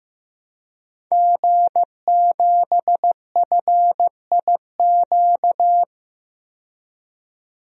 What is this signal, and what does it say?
Morse code, heard as a keyed tone, about 15 wpm, 700 Hz. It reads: G7FIQ
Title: DXCC-100 Call Signs - 5 Characters - 15wpm